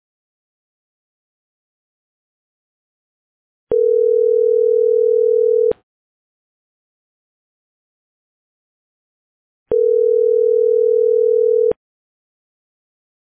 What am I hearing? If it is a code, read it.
Ringback tone (North America), 440 + 480 Hz.